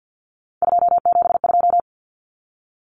Morse code, heard as a keyed tone, about 48 wpm, 700 Hz. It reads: 272
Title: Numbers - 3-Digit 50wpm